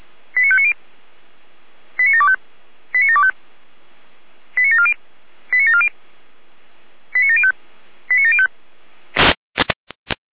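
Five-tone selective call (ZVEI), heard as quick runs of five tones, tones of about 70 ms. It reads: 89590 89725 89725 89740 89740 89705 89705
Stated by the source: ZVEI five-tone selective call received on an SDR.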